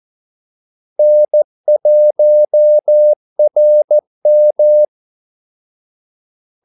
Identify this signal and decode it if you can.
Morse code, heard as a keyed tone, about 14 wpm, 600 Hz. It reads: N1RM
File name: Callsigns-14wpm